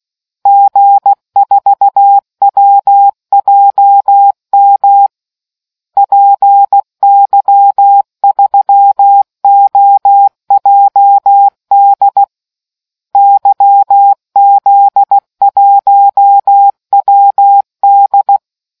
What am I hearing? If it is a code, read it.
Morse code, heard as a keyed tone, about 16 wpm, 800 Hz. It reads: G4WJM PY3OJD YZ1WD